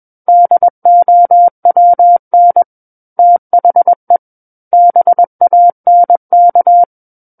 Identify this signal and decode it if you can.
Morse code, heard as a keyed tone, about 21 wpm, 700 Hz. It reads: DOWN THE BANK
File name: Book One / 12-Chapter 12 - 21 WPM